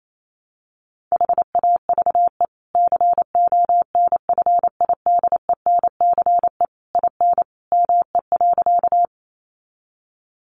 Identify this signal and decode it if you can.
Morse code, heard as a keyed tone, about 28 wpm, 700 Hz. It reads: HAVE CONFIDENCE IN ME.
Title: Sentences from Top 1500 Words 28wpm